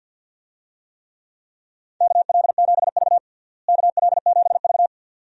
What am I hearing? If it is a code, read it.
Morse code, heard as a keyed tone, about 50 wpm, 700 Hz. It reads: KL7V KL7V
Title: DXCC-100 Call Signs - 4 Characters - Repeated 2x - 50wpm